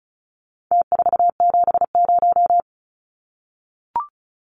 Morse code, heard as a keyed tone, about 34 wpm, 700 Hz. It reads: T470
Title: Numbers - Contest Style - 4-Digit 35wpm